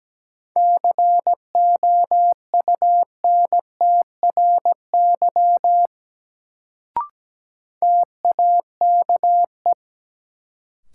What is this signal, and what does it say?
Morse code, heard as a keyed tone, about 17 wpm, 700 Hz. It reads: COUNTRY TAKE
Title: Sets of 2 Words - New Word Drill - 17wpm